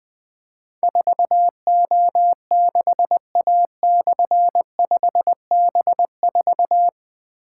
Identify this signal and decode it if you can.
Morse code, heard as a keyed tone, about 20 wpm, 700 Hz. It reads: 4O6A/5B4
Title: Difficult Call Signs 20wpm